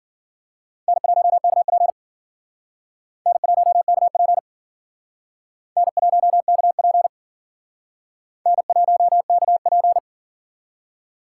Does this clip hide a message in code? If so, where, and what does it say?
Yes: Morse code from 0.9 to 10.0 s, heard as a keyed tone, about 52 wpm, 700 Hz. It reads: N1KP N1KP N1KP N1KP